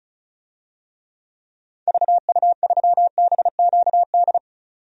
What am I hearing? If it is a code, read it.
Morse code, heard as a keyed tone, about 35 wpm, 700 Hz. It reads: VU3BQD
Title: DXCC-100 Call Signs - 6 Characters - 35wpm